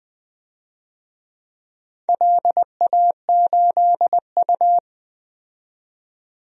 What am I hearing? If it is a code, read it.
Morse code, heard as a keyed tone, about 20 wpm, 700 Hz. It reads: LA8U
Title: DXCC-100 Call Signs - 4 Characters - 20wpm